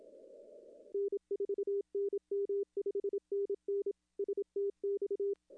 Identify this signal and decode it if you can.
Morse code, heard as a keyed tone, about 27 wpm, 410 Hz. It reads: N4NM 5NN STX
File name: contests / arrl-160-k5tr-op-w5tn -0400